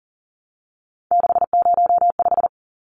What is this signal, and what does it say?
Morse code, heard as a keyed tone, about 40 wpm, 700 Hz. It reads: 605
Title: Numbers - 3-Digit 40wpm